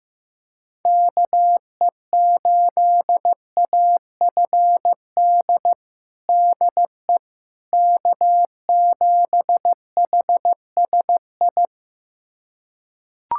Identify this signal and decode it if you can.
Morse code, heard as a keyed tone, about 15 wpm, 700 Hz. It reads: KE8AFD DE K7HSI
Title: USA Callsign DE Callsign 15wpm